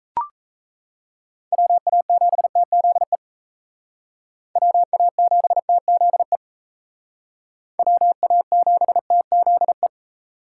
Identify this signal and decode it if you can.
Morse code, heard as a keyed tone, about 35 wpm, 700 Hz. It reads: WA7TZE WA7TZE WA7TZE
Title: Speed Racing - USA Call Signs 28wpm